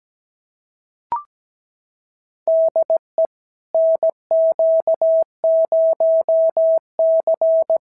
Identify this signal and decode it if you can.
Morse code, heard as a keyed tone, about 17 wpm, 650 Hz. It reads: DE NQ0C
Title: SOTA QSO Practice 17wpm